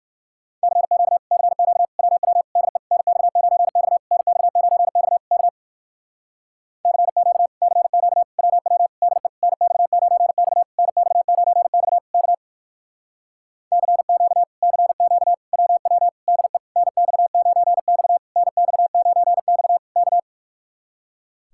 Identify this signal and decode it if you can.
Morse code, heard as a keyed tone, about 63 wpm, 700 Hz. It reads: CQ CQ WW DE NX9X NX9X K CQ CQ WW DE NX9X NX9X K CQ CQ WW DE NX9X NX9X K